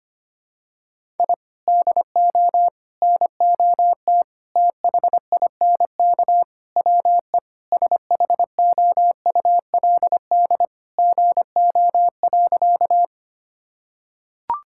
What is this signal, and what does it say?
Morse code, heard as a keyed tone, about 25 wpm, 700 Hz. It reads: I DO NOT THINK WE SHOULD GO.
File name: practice / 25wpm / Sentences with 4-Word Phrases - 25wpm